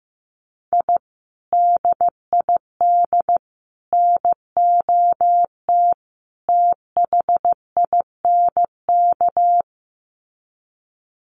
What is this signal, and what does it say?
Morse code, heard as a keyed tone, about 15 wpm, 700 Hz. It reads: I DID NOT THINK
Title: Sets of 4 Words 15wpm